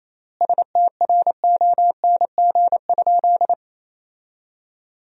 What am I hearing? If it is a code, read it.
Morse code, heard as a keyed tone, about 28 wpm, 700 Hz. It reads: STRONG?
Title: 3x Word-Spacing - Sentences from Top 500 Words - 28wpm